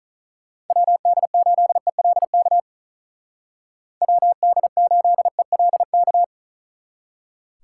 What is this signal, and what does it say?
Morse code, heard as a keyed tone, about 44 wpm, 700 Hz. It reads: WD8ELK WD8ELK